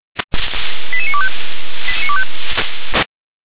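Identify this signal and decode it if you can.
Five-tone selective call (ZVEI), heard as quick runs of five tones, tones of about 70 ms. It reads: 88925
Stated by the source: Five-tone ZVEI selective call, received with an SDR.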